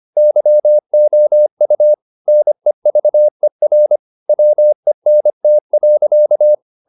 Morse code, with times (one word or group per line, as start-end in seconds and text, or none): 0.2-1.9 YOU
2.3-4.0 NEVER
4.3-6.5 WENT.